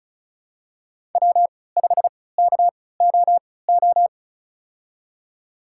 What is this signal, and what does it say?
Morse code, heard as a keyed tone, about 36 wpm, spaced as for 20 wpm, 700 Hz. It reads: W5KOO
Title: USA Call Signs - 5 Characters - 3x Character Spacing - 35wpm